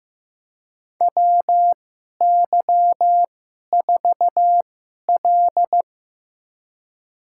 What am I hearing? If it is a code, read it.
Morse code, heard as a keyed tone, about 15 wpm, spaced as for 11 wpm, 700 Hz. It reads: WY4L
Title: USA Call Signs - 2x Character Spacing - 15wpm